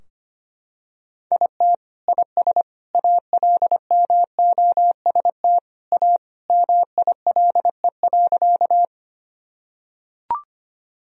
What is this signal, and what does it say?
Morse code, heard as a keyed tone, about 25 wpm, 700 Hz. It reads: IT IS ALMOST A MILE.
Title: Sentences from Top 1000 Words - Review Part 4 - 25wpm